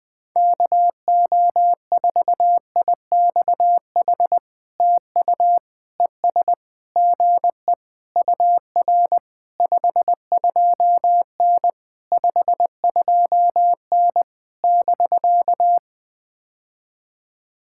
Morse code, heard as a keyed tone, about 20 wpm, 700 Hz. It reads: KO4IXH TU ES GE UR 52N 52N <BK>
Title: POTA QSO Practice - 20wpm